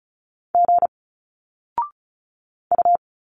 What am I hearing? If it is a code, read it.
Morse code, heard as a keyed tone, about 36 wpm, 700 Hz. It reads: G U
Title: Single Letters 35wpm